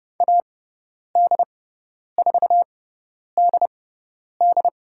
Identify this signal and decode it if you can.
Morse code, heard as a keyed tone, about 30 wpm, spaced as for 10 wpm, 700 Hz. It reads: AD4DD
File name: Lesson 16g - Letter c - Call Signs 30wpm